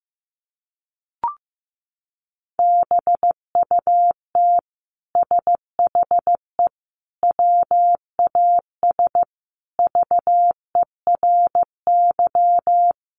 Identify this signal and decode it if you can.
Morse code, heard as a keyed tone, about 15 wpm, 700 Hz. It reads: BUT SHE WAS VERY
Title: Sentences from Top 1000 Words 15wpm